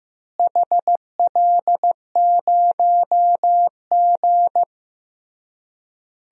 Morse code, heard as a keyed tone, about 16 wpm, 700 Hz. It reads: HL0G
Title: DXCC-100 Call Signs - 4 Characters - Repeated 3x - 15wpm